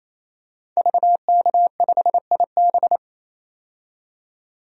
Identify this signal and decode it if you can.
Morse code, heard as a keyed tone, about 28 wpm, 700 Hz. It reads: VK5IB
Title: DXCC-100 Call Signs - 5 Characters - 28wpm